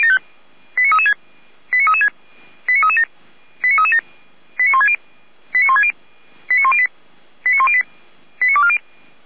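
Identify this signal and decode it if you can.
Five-tone selective call (ZVEI), heard as quick runs of five tones, tones of about 80 ms. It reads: 89336 89336 89337 89337 89160 89160 89108 89108 89240
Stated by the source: Five-tone ZVEI selective call received by SDR, code 89165.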